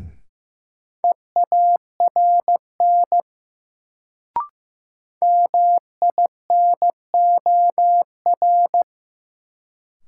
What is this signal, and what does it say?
Morse code, heard as a keyed tone, about 15 wpm, 700 Hz. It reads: EARN MINOR
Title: Sentences from Top 1500 Words - New Word Drill - 15wpm